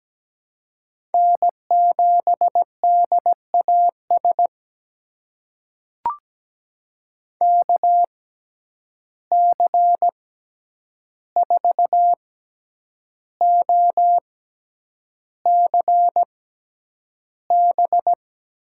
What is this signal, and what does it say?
Morse code, heard as a keyed tone, about 17 wpm, spaced as for 8 wpm, 700 Hz. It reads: N7DAS KC4OCB
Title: USA Call Signs - 6x Character Spacing - 17wpm